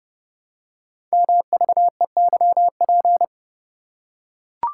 Morse code, heard as a keyed tone, about 30 wpm, 700 Hz. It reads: MVEYP
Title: Code Groups - 5 Letters - 30wpm